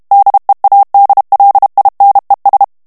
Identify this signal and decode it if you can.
Morse code, heard as a keyed tone, about 32 wpm, 800 Hz. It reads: DEADLINES
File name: morse mp3 / ANGLAIS texte clair 3 vitesse 30 mots minute